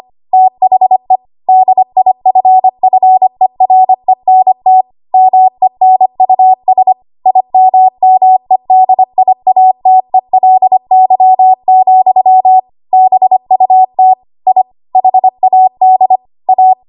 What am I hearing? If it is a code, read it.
Morse code, heard as a keyed tone, about 25 wpm, 750 Hz. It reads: THE DIFFERENT MENUS IMMEDIATELY, BUT I HAD A
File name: Archive / 25 WPM / 25WPM